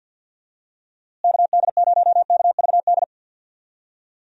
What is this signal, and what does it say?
Morse code, heard as a keyed tone, about 53 wpm, 700 Hz. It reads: KD0KVD
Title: USA Call Signs - 6 Characters - Repeated 2x - 50wpm